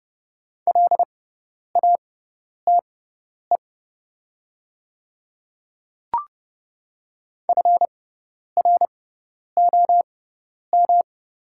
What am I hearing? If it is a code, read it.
Morse code, heard as a keyed tone, about 30 wpm, spaced as for 10 wpm, 700 Hz. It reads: LATE FROM
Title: 6x Character Spacing - Top 500 Words - 30wpm